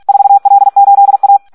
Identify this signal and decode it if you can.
Morse code, heard as a keyed tone, about 46 wpm, 800 Hz. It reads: 4L8A
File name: Audio practica CW / SD 1 46